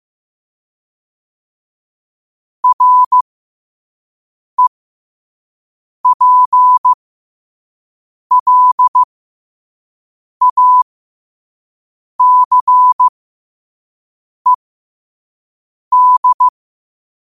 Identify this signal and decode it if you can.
Morse code, heard as a keyed tone, about 15 wpm, spaced as for 5 wpm, 1000 Hz. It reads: REPLACED
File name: qotdfiles / 5wpm